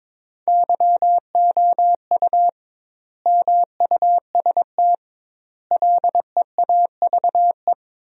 Morse code, heard as a keyed tone, about 22 wpm, 700 Hz. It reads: YOU MUST LEAVE
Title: 2x Word-Spacing - Sentences from Top 400 Words - 22wpm